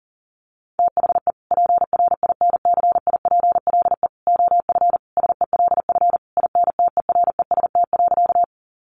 Morse code, heard as a keyed tone, about 40 wpm, 700 Hz. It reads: THE PRINCIPLE OF SELF INTEREST.
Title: Sentences from Top 1400 Words 40wpm